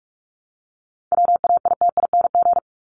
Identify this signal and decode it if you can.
Morse code, heard as a keyed tone, about 44 wpm, 700 Hz. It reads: WAITING
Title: ING Suffix - Top 100 Words with Base Words 45wpm